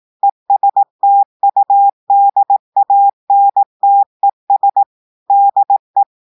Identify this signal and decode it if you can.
Morse code, heard as a keyed tone, about 18 wpm, 800 Hz. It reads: ESTUDANTES DE
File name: audios / portugais / portugais texte en clair 1 vitesse 18 mots minutes